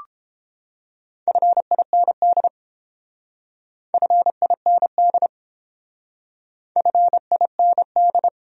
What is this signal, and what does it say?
Morse code, heard as a keyed tone, about 30 wpm, 700 Hz. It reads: FIND FIND FIND